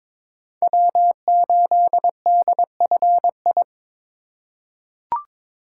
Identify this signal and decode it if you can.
Morse code, heard as a keyed tone, about 22 wpm, 700 Hz. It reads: W8DFI
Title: USA Call Signs - 5 Characters - 4x Character Spacing - 22wpm